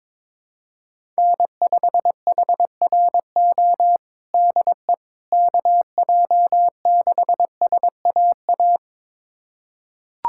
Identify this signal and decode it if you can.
Morse code, heard as a keyed tone, about 22 wpm, 700 Hz. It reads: N5HRO DE KJ6SAA
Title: USA Callsign DE Callsign 22wpm